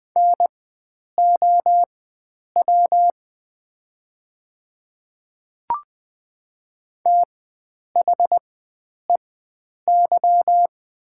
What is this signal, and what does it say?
Morse code, heard as a keyed tone, about 20 wpm, spaced as for 9 wpm, 700 Hz. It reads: NOW THEY
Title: 4x Character Spacing - Top 100 Words - 20wpm